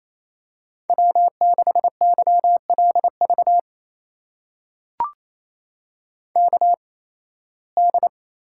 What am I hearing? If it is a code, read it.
Morse code, heard as a keyed tone, about 28 wpm, spaced as for 18 wpm, 700 Hz. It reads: W6YLV KD
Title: USA Call Signs - 8x Character Spacing - 28wpm